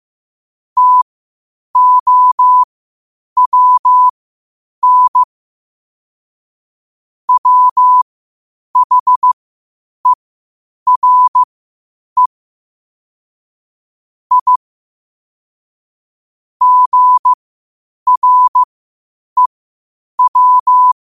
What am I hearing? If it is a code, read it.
Morse code, heard as a keyed tone, about 15 wpm, spaced as for 8 wpm, 1000 Hz. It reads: TOWN WHERE I GREW
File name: qotdfiles / 8wpm